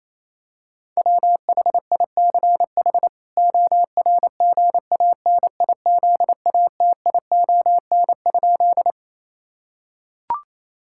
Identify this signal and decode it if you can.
Morse code, heard as a keyed tone, about 28 wpm, 700 Hz. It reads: WHICH ORGANIZATION?
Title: Sentences from Top 1000 Words - Review Part 5 - 28wpm